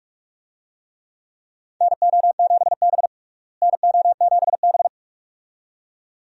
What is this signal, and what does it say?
Morse code, heard as a keyed tone, about 42 wpm, 700 Hz. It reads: NO7B NO7B
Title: USA Call Signs - 4 Characters - Repeated 2x - 45wpm